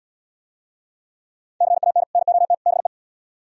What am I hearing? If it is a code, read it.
Morse code, heard as a keyed tone, about 76 wpm, 700 Hz. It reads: CAN NOT BE